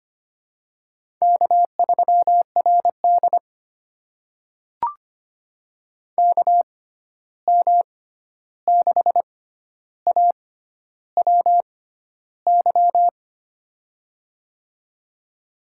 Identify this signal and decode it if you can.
Morse code, heard as a keyed tone, about 25 wpm, spaced as for 11 wpm, 700 Hz. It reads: K3RD KM6AWY
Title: USA Call Signs - 6x Character Spacing - 25wpm